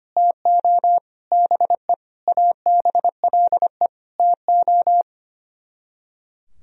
Morse code, heard as a keyed tone, about 33 wpm, 700 Hz. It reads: TO BE ABLE TO